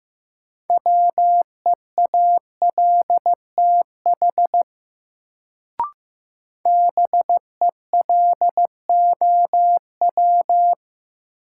Morse code, heard as a keyed tone, about 15 wpm, 700 Hz. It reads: WEALTH BELOW
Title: Sentences from Top 1200 Words - New Word Drill Alt - 15wpm